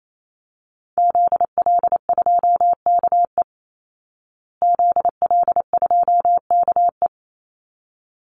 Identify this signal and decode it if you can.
Morse code, heard as a keyed tone, about 28 wpm, 700 Hz. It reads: ZL2KE ZL2KE
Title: Call Signs Repeated 3x - 28wpm